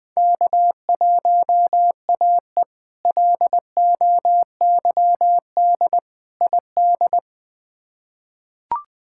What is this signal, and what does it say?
Morse code, heard as a keyed tone, about 20 wpm, 700 Hz. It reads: K1AE LOYD ID
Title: K1USN SST QSO Practice - 20wpm